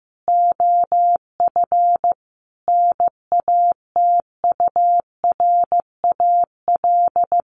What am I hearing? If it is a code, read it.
Morse code, heard as a keyed tone, about 15 wpm, 700 Hz. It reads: OF NATURAL
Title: Sentences from Top 1400 Words 15wpm